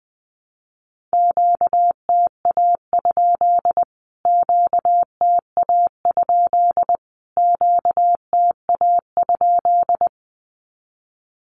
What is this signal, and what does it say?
Morse code, heard as a keyed tone, about 20 wpm, 700 Hz. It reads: QTA? QTA? QTA?
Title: Q-Codes Repeated 3x 20wpm